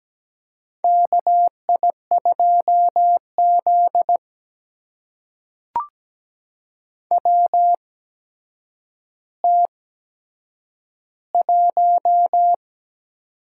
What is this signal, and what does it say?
Morse code, heard as a keyed tone, about 17 wpm, spaced as for 8 wpm, 700 Hz. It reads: KI2Z WT1
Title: USA Call Signs - 4 Characters - 8x Character Spacing - 17wpm